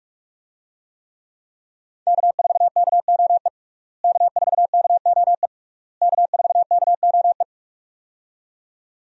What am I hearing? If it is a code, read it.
Morse code, heard as a keyed tone, about 42 wpm, 700 Hz. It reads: K4KOE K4KOE K4KOE
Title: USA Call Signs - 5 Characters - Repeated 3x - 45wpm